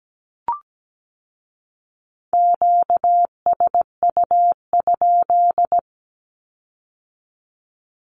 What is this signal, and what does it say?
Morse code, heard as a keyed tone, about 17 wpm, 700 Hz. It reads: QSU?
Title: Q-Codes with Spoken Meaning - 17wpm